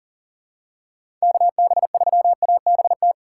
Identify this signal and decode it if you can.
Morse code, heard as a keyed tone, about 44 wpm, 700 Hz. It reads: KB3ABT
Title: USA Call Signs - 6 Characters - 6x Character Spacing - 40wpm